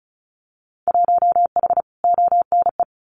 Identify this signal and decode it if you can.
Morse code, heard as a keyed tone, about 34 wpm, 700 Hz. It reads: 1H ONE
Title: Winter Field Day Exchanges Encore 35wpm